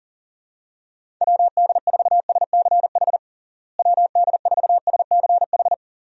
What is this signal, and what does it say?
Morse code, heard as a keyed tone, about 40 wpm, 700 Hz. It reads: WD4SCH WD4SCH
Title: USA Call Signs - 6 Characters - Repeated 2x - 40wpm